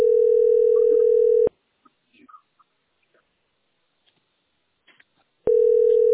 Ringback tone (North America), 440 + 480 Hz.